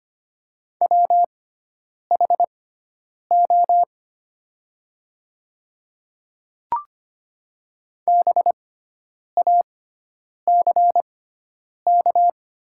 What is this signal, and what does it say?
Morse code, heard as a keyed tone, about 25 wpm, spaced as for 9 wpm, 700 Hz. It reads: WHO BACK
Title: 6x Character Spacing - Top 300 Words - 25wpm